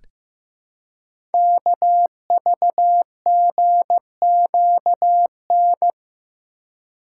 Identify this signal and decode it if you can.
Morse code, heard as a keyed tone, about 15 wpm, 700 Hz. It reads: KVGQN
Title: Code Groups - 5 Letters - 15wpm